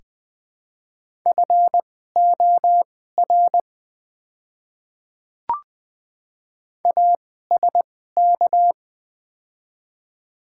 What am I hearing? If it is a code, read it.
Morse code, heard as a keyed tone, about 20 wpm, spaced as for 14 wpm, 700 Hz. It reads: FOR ASK
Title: 2x Character Spacing - Top 300 Words - 20wpm